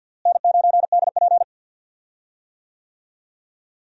Morse code, heard as a keyed tone, about 46 wpm, 700 Hz. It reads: N9DP
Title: USA Call Signs - 4 Characters - Repeated 3x - 50wpm